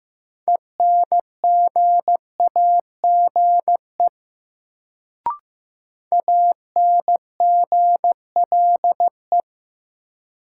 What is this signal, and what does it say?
Morse code, heard as a keyed tone, about 16 wpm, 700 Hz. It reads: ENGAGE ANGLE
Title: Sentences from Top 1500 Words - New Word Drill - 15wpm